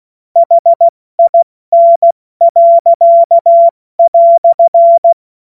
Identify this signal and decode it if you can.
Morse code, heard as a keyed tone, about 16 wpm, 680 Hz. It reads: HIN."